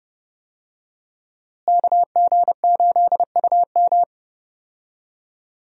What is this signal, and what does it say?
Morse code, heard as a keyed tone, about 31 wpm, 700 Hz. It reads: KG8UM
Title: USA Call Signs - 5 Characters - Repeated 2x - 30wpm